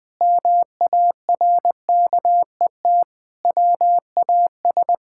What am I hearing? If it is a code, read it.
Morse code, heard as a keyed tone, about 20 wpm, 700 Hz. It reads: MARKET WAS
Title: Sentences with 5-Word Phrases - Spoken Beginning - 20wpm